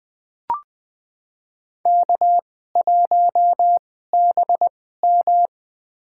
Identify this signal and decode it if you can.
Morse code, heard as a keyed tone, about 20 wpm, spaced as for 15 wpm, 700 Hz. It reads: K1BM
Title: USA Call Signs - 2x Character Spacing - 20wpm